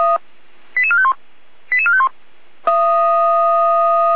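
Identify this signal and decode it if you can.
Five-tone selective call (ZVEI), heard as quick runs of five tones, tones of about 70 ms. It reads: 88451 88451
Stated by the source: ZVEI five-tone call received by SDR, from Feuerwehr Orlamünde audio server.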